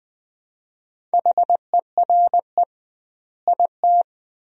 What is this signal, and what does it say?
Morse code, heard as a keyed tone, about 20 wpm, 700 Hz. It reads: HERE IT
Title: 2x Word-Spacing - Sentences from Top 200 Words - 20wpm